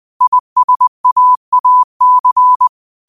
Morse code, heard as a keyed tone, about 20 wpm, 1000 Hz. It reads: ISAAC
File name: qotdfiles / 20wpm